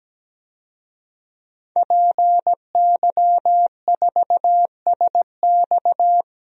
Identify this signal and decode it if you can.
Morse code, heard as a keyed tone, about 17 wpm, 700 Hz. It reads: PY4SX
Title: DXCC-100 Call Signs - 5 Characters - 17wpm